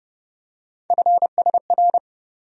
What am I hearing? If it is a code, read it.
Morse code, heard as a keyed tone, about 30 wpm, 700 Hz. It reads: FSR
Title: USA Call Sign Postfixes - 3 Letters 30wpm